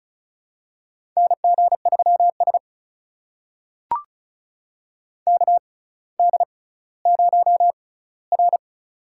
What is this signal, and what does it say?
Morse code, heard as a keyed tone, about 36 wpm, spaced as for 18 wpm, 700 Hz. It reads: NG3S KD0R
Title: USA Call Signs - 6x Character Spacing - 35wpm